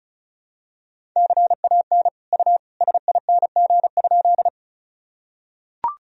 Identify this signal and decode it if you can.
Morse code, heard as a keyed tone, about 36 wpm, 700 Hz. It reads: CAN U SING?
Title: Sentences from Top 400 Words and Abbreviations 35wpm